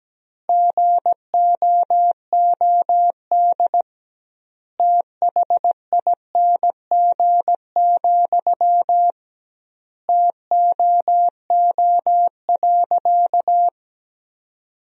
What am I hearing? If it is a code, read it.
Morse code, heard as a keyed tone, about 17 wpm, 700 Hz. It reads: GOOD THING, TOO.
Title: 2x Word-Spacing - Sentences from Top 200 Words - 17wpm